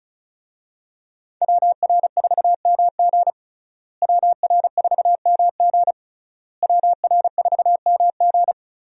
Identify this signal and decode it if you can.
Morse code, heard as a keyed tone, about 36 wpm, 700 Hz. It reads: WR4MG WR4MG WR4MG
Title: USA Call Signs - 5 Characters - Repeated 3x - 35wpm